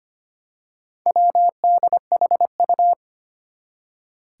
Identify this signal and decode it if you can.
Morse code, heard as a keyed tone, about 25 wpm, 700 Hz. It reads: WDHU
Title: Four Challenging Letter-Number Combinations 25wpm